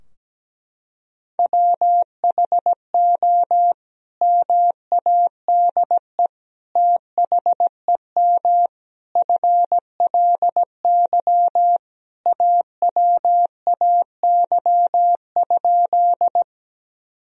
Morse code, heard as a keyed tone, about 17 wpm, 700 Hz. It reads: WHO MADE THEM FLY AWAY?
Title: Sentences from Top 1000 Words - Review Part 2 - 17wpm